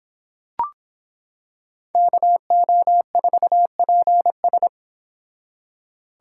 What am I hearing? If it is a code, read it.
Morse code, heard as a keyed tone, about 26 wpm, 700 Hz. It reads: KO4PS